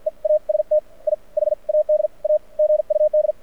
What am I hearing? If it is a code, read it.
Morse code, heard as a keyed tone, about 52 wpm, spaced as for 42 wpm, 600 Hz. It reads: EAST I HAD A GUD
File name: archived / CRSnet